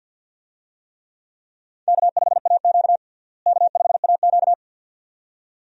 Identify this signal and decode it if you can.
Morse code, heard as a keyed tone, about 52 wpm, 700 Hz. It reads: K5AQ K5AQ
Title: USA Call Signs - 4 Characters - Repeated 2x - 50wpm